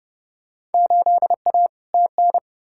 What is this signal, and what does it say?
Morse code, heard as a keyed tone, about 30 wpm, 700 Hz. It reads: 8A TN
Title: Field Day Exchanges Encore - 30wpm